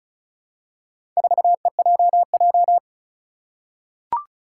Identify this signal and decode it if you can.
Morse code, heard as a keyed tone, about 36 wpm, 700 Hz. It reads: 4EJJ
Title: Four Challenging Letter-Number Combinations 35wpm